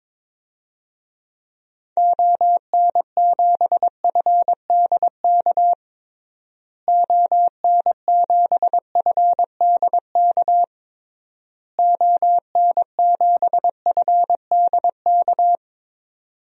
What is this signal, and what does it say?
Morse code, heard as a keyed tone, about 22 wpm, 700 Hz. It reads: ON7FDK ON7FDK ON7FDK
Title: DXCC-100 Call Signs - 6 Characters - Repeated 3x - 22wpm